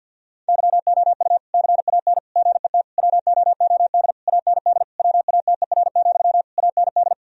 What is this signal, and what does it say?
Morse code, heard as a keyed tone, about 50 wpm, 700 Hz. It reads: YOU CAN GET WOOD AND WATER, AND
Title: Sentences from Top 1000 Words - Review Part 2 - 50wpm